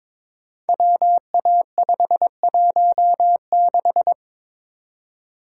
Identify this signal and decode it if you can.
Morse code, heard as a keyed tone, about 23 wpm, 700 Hz. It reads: WA516
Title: Code Groups - 5 Letter:Numbers - 22wpm